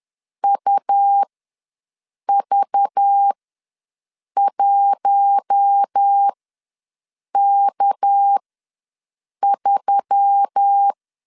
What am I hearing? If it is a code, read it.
Morse code, heard as a keyed tone, about 10 wpm, spaced as for 6 wpm, 800 Hz. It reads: UV1K3
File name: LEZIONI / 6 PKX0 / 6 PK ws 10wpm 3745